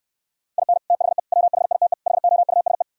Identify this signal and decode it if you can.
Morse code, heard as a keyed tone, about 67 wpm, 700 Hz. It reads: IN THE WHITE HOUSE